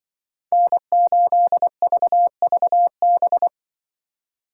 Morse code, heard as a keyed tone, about 25 wpm, 700 Hz. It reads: N8VVB
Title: Speed Racing - USA Call Signs 20wpm